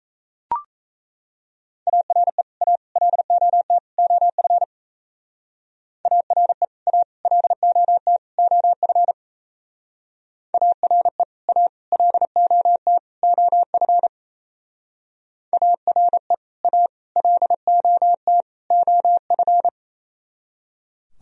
Morse code, heard as a keyed tone, about 35 wpm, 700 Hz. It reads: ARE A LOT OF ARE A LOT OF ARE A LOT OF ARE A LOT OF
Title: Speed-Racing - Sets of 4 Words 28wpm